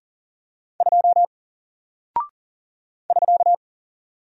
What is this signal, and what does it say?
Morse code, heard as a keyed tone, about 40 wpm, 700 Hz. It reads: 2 <SK>
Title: Single Character (Letter:Number:Punctuation:Prosign) Alt 40wpm